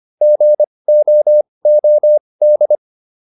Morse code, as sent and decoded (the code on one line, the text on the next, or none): --. --- --- -..
GOOD